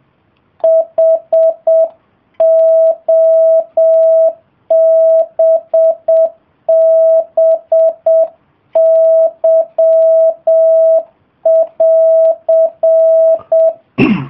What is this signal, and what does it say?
Morse code, heard as a keyed tone, about 7 wpm, 650 Hz. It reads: HOBBY<AR>